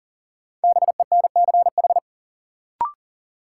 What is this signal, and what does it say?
Morse code, heard as a keyed tone, about 37 wpm, 700 Hz. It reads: BENCH
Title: Sentences from Top 1400 Words - New Word Drill Alt - 40wpm